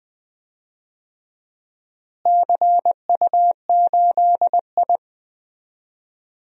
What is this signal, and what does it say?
Morse code, heard as a keyed tone, about 20 wpm, 700 Hz. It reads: CU8I
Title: DXCC-100 Call Signs - 4 Characters - Repeated 3x - 20wpm